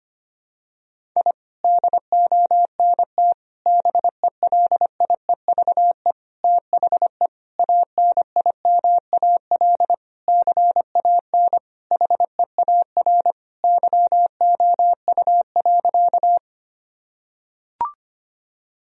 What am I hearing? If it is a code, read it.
Morse code, heard as a keyed tone, about 25 wpm, 700 Hz. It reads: I DONT BELIEVE THE ANIMAL CAN HEAR YOU.
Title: Sentences from Top 1000 Words - Review Part 1 - 25wpm